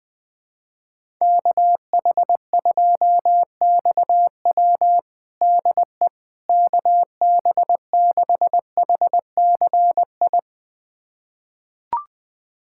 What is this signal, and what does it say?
Morse code, heard as a keyed tone, about 20 wpm, 700 Hz. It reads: KH2XW DE KB6HCI
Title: USA Callsign DE Callsign 20wpm